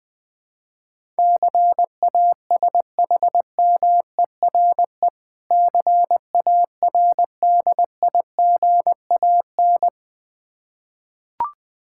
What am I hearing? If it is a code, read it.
Morse code, heard as a keyed tone, about 20 wpm, 700 Hz. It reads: CASHMERE CARDIGAN
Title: Difficult Words with Hints - 20wpm